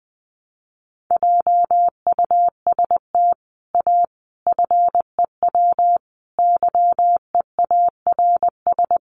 Morse code, heard as a keyed tone, about 20 wpm, 700 Hz. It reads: JUST A FEW YEARS
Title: Sets of 5 Words Encore 20wpm